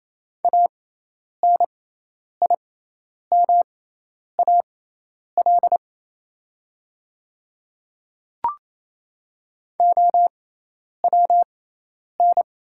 Morse code, heard as a keyed tone, about 28 wpm, spaced as for 10 wpm, 690 Hz. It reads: ANIMAL OWN
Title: 6x Character Spacing - Top 300 Words - 28wpm